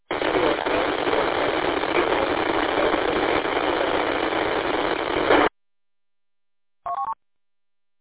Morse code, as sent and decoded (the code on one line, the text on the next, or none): none